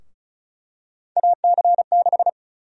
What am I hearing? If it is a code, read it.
Morse code, heard as a keyed tone, about 34 wpm, 700 Hz. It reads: AC6
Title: USA Call Sign Prefixes - 35wpm